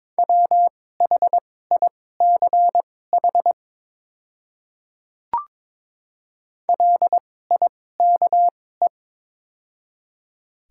Morse code, heard as a keyed tone, about 22 wpm, spaced as for 16 wpm, 700 Hz. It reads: WHICH LIKE
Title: 2x Character Spacing - Top 100 Words - 22wpm